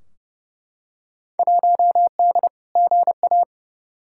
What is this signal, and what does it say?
Morse code, heard as a keyed tone, about 30 wpm, 700 Hz. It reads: 1D GA
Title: Field Day Exchanges - 30wpm